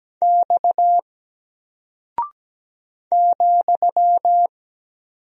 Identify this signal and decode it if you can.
Morse code, heard as a keyed tone, about 17 wpm, 700 Hz. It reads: X ,
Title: Single Character (Letter:Number:Punctuation:Prosign) 17wpm